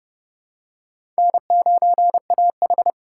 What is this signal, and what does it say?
Morse code, heard as a keyed tone, about 29 wpm, 700 Hz. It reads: N9AH
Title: USA Call Signs - 4 Characters - 8x Character Spacing - 30wpm